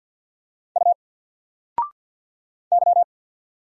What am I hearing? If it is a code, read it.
Morse code, heard as a keyed tone, about 50 wpm, 700 Hz. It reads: U Y
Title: Single Letter-Number 50wpm